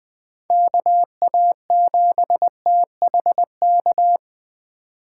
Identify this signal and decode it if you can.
Morse code, heard as a keyed tone, about 24 wpm, 700 Hz. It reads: KA7THK
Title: Speed Racing - USA 2x3 Call Signs 20wpm